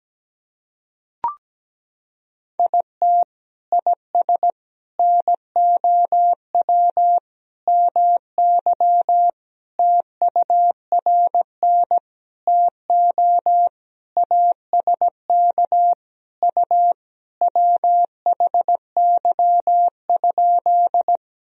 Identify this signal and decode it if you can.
Morse code, heard as a keyed tone, about 17 wpm, 700 Hz. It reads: IT IS NOW MY TURN TO ASK U WHY?
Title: Sentences from Top 200 Words and Abbreviations 17wpm